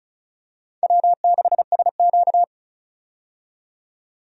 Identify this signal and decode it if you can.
Morse code, heard as a keyed tone, about 34 wpm, 700 Hz. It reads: W6SQ
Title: USA Call Signs - 4 Characters - 35wpm